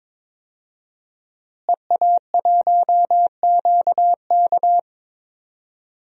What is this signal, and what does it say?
Morse code, heard as a keyed tone, about 21 wpm, 700 Hz. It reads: EA1QK
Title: DXCC-100 Call Signs - 5 Characters - Repeated 3x - 22wpm